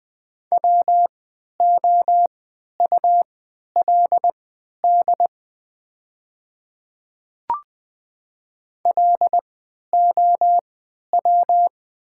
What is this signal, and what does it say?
Morse code, heard as a keyed tone, about 20 wpm, spaced as for 11 wpm, 700 Hz. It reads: WOULD LOW
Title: 3x Character Spacing - Top 500 Words - 20wpm